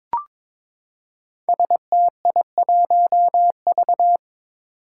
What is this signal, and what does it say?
Morse code, heard as a keyed tone, about 23 wpm, 700 Hz. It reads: STI1V
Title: Five Challenging Letter-Number Combinations 22wpm